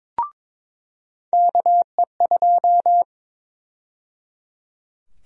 Morse code, heard as a keyed tone, about 22 wpm, 700 Hz. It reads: KE2
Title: USA Call Sign Prefixes - 22wpm